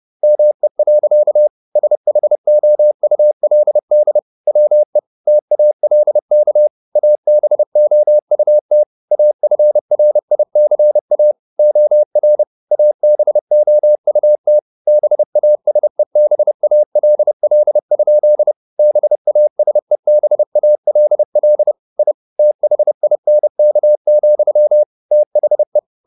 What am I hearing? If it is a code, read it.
Morse code, heard as a keyed tone, about 30 wpm, 600 Hz. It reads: ME. SHOULD WE TALK ABOUT AFRICA OR ABOUT BASEBALL? BASEBALL I THINK, THE
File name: the old man and the sea / the old man and the sea 1 30wpm